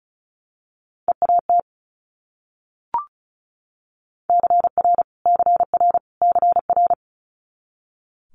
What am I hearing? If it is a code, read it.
Morse code, heard as a keyed tone, about 36 wpm, 700 Hz. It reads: EAT CR CR CR
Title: Multi-letter Phonemes Repeated 3x 35wpm